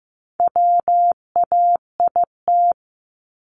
Morse code, heard as a keyed tone, about 15 wpm, 700 Hz. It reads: WAIT
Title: ING Suffix - Top 100 Words with Base Words 15wpm